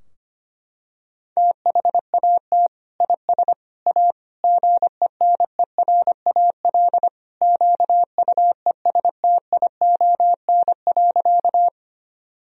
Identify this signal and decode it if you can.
Morse code, heard as a keyed tone, about 25 wpm, 700 Hz. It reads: THAT IS A GENERAL QUESTION.
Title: Sentences from Top 1000 Words - Review Part 3 - 25wpm